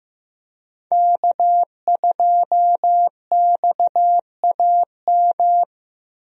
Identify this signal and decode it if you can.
Morse code, heard as a keyed tone, about 15 wpm, 700 Hz. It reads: K2XAM
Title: USA Call Signs - 5 Characters - Repeated 3x - 15wpm